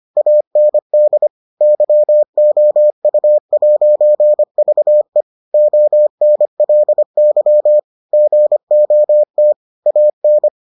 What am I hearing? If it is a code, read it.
Morse code, heard as a keyed tone, about 25 wpm, 600 Hz. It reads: AND YOU'VE ONLY GOT AN